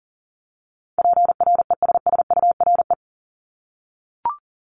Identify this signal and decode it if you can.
Morse code, heard as a keyed tone, about 42 wpm, 700 Hz. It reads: PRESSURE
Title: Multi-letter Phonemes 40wpm